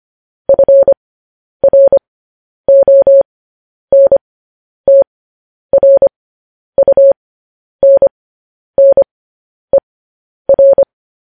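Morse code, heard as a keyed tone, about 25 wpm, spaced as for 10 wpm, 550 Hz. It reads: FRONTRUNNER
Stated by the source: Morse code news headlines practice transmission.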